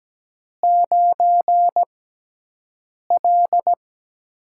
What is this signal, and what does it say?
Morse code, heard as a keyed tone, about 17 wpm, spaced as for 6 wpm, 700 Hz. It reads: 9L
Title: USA Call Signs - 4 Characters - 6x Character Spacing - 17wpm